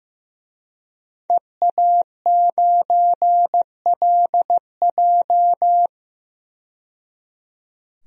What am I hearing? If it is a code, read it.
Morse code, heard as a keyed tone, about 15 wpm, 700 Hz. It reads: EA9LJ
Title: DXCC-100 Call Signs - 5 Characters - 15wpm